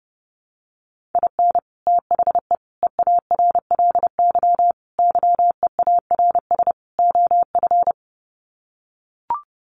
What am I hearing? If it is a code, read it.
Morse code, heard as a keyed tone, about 30 wpm, 700 Hz. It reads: IN THE EARLY YEARS OF
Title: Sets of 5 Words Encore 30wpm